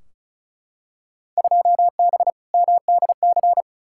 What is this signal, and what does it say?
Morse code, heard as a keyed tone, about 34 wpm, 700 Hz. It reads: 2B MDC
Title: Field Day Exchanges - 35wpm